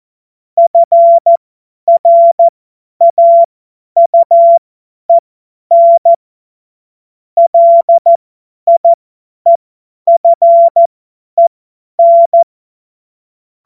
Morse code, heard as a keyed tone, about 14 wpm, spaced as for 10 wpm, 680 Hz. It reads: FRAUEN LIEFEN
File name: Morsekurs / Witze / Witz 30. August 2022 50 F70